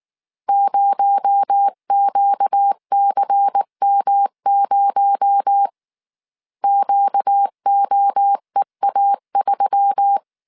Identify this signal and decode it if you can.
Morse code, heard as a keyed tone, about 19 wpm, 800 Hz. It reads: 0QCM0 QOEA3
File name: CW / DET / 1 6 1500C S C 18wpm 2356